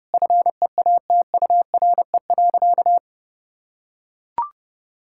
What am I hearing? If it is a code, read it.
Morse code, heard as a keyed tone, about 32 wpm, 700 Hz. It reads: FEATURE.
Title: Sentences from Top 1000 Words - Review Part 5 - 30wpm